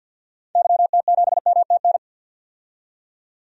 Morse code, heard as a keyed tone, about 46 wpm, 700 Hz. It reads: YT7MTN
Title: DXCC-100 Call Signs - 6 Characters - 50wpm